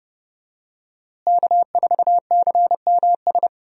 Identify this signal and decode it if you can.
Morse code, heard as a keyed tone, about 31 wpm, 700 Hz. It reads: K4CMS
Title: USA Call Signs - 5 Characters - Repeated 2x - 30wpm